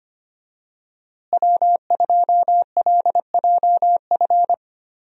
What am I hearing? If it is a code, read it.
Morse code, heard as a keyed tone, about 25 wpm, 700 Hz. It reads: W2LJF
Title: USA Call Signs - 5 Characters - Repeated 2x - 25wpm